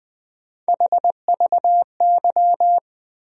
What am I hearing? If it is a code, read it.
Morse code, heard as a keyed tone, about 20 wpm, 700 Hz. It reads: HVY
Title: USA Call Sign Postfixes - 3 Letters 20wpm